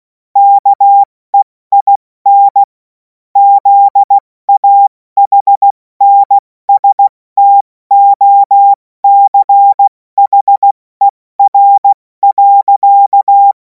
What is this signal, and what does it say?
Morse code, heard as a keyed tone, about 16 wpm, 800 Hz. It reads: KEIN ZAHNSTOCHER.
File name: Morsekurs / Witze / Witz 17. März 2026 70 F80